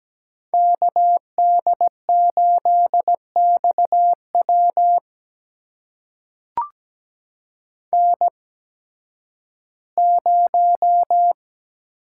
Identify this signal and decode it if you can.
Morse code, heard as a keyed tone, about 17 wpm, spaced as for 11 wpm, 700 Hz. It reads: KD8XW N0
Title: USA Call Signs - 5 Characters - 8x Character Spacing - 17wpm